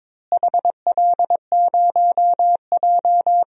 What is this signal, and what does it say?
Morse code, heard as a keyed tone, about 22 wpm, 700 Hz. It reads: HL0J
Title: DXCC-100 Call Signs - 4 Characters - Repeated 3x - 22wpm